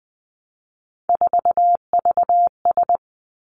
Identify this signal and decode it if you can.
Morse code, heard as a keyed tone, about 20 wpm, 700 Hz. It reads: 4VS
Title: Three Challenging Letter-Number Combinations 20wpm